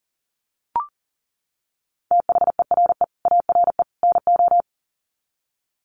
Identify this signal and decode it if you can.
Morse code, heard as a keyed tone, about 40 wpm, 700 Hz. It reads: THERE ARE NO
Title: Sets of 3 Words 40wpm